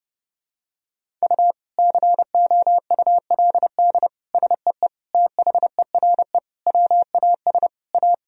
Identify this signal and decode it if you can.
Morse code, heard as a keyed tone, about 30 wpm, 700 Hz. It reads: U COULD SEE THERE WAS A
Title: Sentences from Top 500 Words and Abbreviations 30wpm